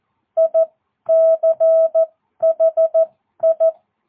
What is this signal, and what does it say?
Morse code, heard as a keyed tone, about 14 wpm, spaced as for 11 wpm, 660 Hz. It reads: ICHI
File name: jh8cbh / 2022cw